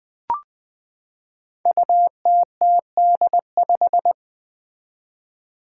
Morse code, heard as a keyed tone, about 21 wpm, 700 Hz. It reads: UTTD5
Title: Five Challenging Letter-Number Combinations 20wpm